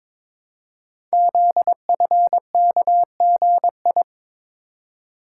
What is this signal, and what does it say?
Morse code, heard as a keyed tone, about 22 wpm, 700 Hz. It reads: ZFKGI
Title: Code Groups - 5 Letters - 22wpm